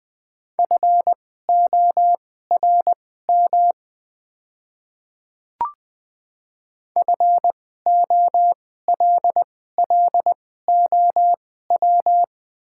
Morse code, heard as a keyed tone, about 20 wpm, spaced as for 14 wpm, 690 Hz. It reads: FORM FOLLOW
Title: 2x Character Spacing - Top 200 Words - 20wpm